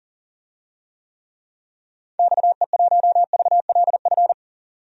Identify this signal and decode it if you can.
Morse code, heard as a keyed tone, about 40 wpm, 700 Hz. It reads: XE1VLF
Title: DXCC-100 Call Signs - 6 Characters - 40wpm